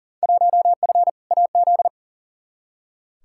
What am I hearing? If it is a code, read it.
Morse code, heard as a keyed tone, about 38 wpm, 700 Hz. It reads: 1F AZ
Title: Field Day Exchanges - 40wpm